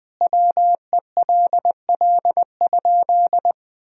Morse code, heard as a keyed tone, about 20 wpm, 700 Hz. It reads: WELL?
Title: Sentences from Top 1000 Words - Review Part 1 - 20wpm